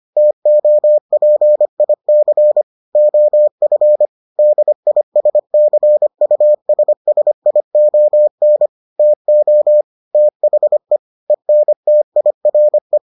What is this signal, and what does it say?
Morse code, heard as a keyed tone, about 25 wpm, 600 Hz. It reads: TOPIC OF DISCUSSION TO THE ENTIRE